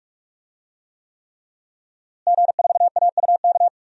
Morse code, heard as a keyed tone, about 48 wpm, 700 Hz. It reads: G4AUK
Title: DXCC-100 Call Signs - 5 Characters - Repeated 3x - 45wpm